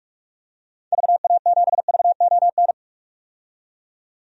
Morse code, heard as a keyed tone, about 41 wpm, 700 Hz. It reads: VA7VON
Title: DXCC-100 Call Signs - 6 Characters - 45wpm